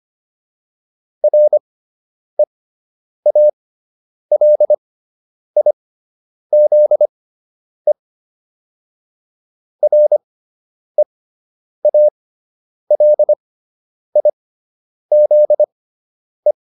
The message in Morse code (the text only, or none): REALIZE REALIZE